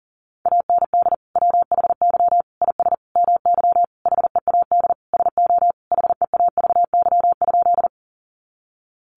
Morse code, heard as a keyed tone, about 40 wpm, 700 Hz. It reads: AND WHY IS MY HEAD SO HEAVY?
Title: Sentences from Top 1100 Words 40wpm